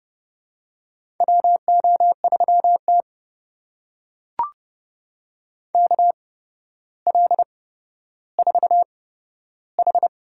Morse code, heard as a keyed tone, about 30 wpm, spaced as for 13 wpm, 700 Hz. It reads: WO3T KL4H
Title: USA Call Signs - 4 Characters - 8x Character Spacing - 30wpm